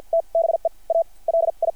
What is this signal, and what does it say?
Morse code, heard as a keyed tone, about 53 wpm, spaced as for 43 wpm, 690 Hz. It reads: T BE A LI
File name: archived / CRSnet